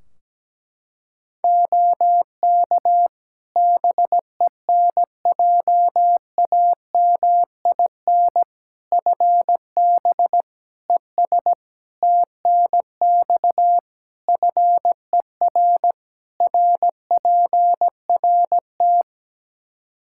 Morse code, heard as a keyed tone, about 17 wpm, 700 Hz. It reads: OK BENJAMIN FB ES TNX FER RPRT
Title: K1USN SST QSO Practice 17wpm